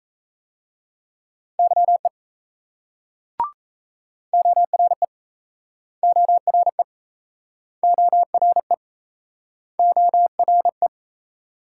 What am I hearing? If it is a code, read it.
Morse code, heard as a keyed tone, about 38 wpm, 700 Hz. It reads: YE ORE ORE ORE ORE